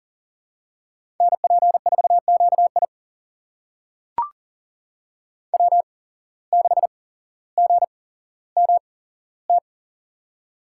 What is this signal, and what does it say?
Morse code, heard as a keyed tone, about 40 wpm, spaced as for 17 wpm, 700 Hz. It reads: NP4QI W6GMT
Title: USA Call Signs - 5 Characters - 8x Character Spacing - 40wpm